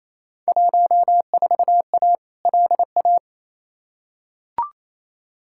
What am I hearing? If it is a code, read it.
Morse code, heard as a keyed tone, about 28 wpm, 700 Hz. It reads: 14A LA
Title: Field Day Exchanges Encore - 28wpm